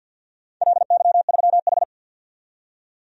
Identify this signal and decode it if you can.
Morse code, heard as a keyed tone, about 53 wpm, 700 Hz. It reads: LY3H
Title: DXCC-100 Call Signs - 4 Characters - 50wpm